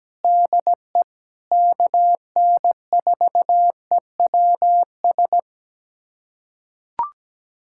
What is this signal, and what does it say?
Morse code, heard as a keyed tone, about 17 wpm, 700 Hz. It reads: DE KN4EWS
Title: USA Callsign DE Callsign 17wpm